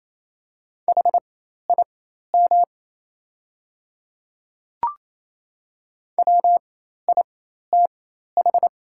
Morse code, heard as a keyed tone, about 28 wpm, spaced as for 13 wpm, 700 Hz. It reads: HIM WITH
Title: 4x Character Spacing - Top 100 Words - 28wpm